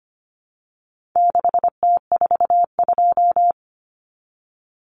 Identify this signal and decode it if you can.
Morse code, heard as a keyed tone, about 25 wpm, 700 Hz. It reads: 6T42
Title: Common Cut Numbers - 4-Digit 25wpm